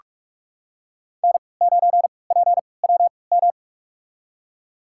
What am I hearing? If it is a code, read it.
Morse code, heard as a keyed tone, about 46 wpm, spaced as for 26 wpm, 700 Hz. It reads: N9PWM
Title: USA Call Signs - 5 Characters - 3x Character Spacing - 45wpm